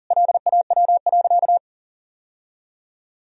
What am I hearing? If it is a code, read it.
Morse code, heard as a keyed tone, about 37 wpm, 700 Hz. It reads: LAW.
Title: Sentences from Top 1000 Words - Review Part 3 - 40wpm